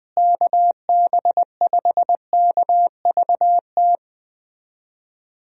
Morse code, heard as a keyed tone, about 20 wpm, 700 Hz. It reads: KB5KVT
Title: USA Call Signs - 6 Characters - Repeated 3x - 20wpm